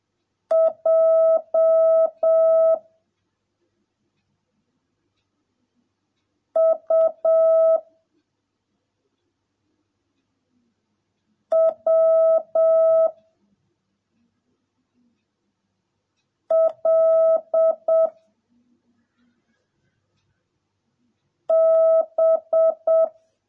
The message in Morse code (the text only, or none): J U W L B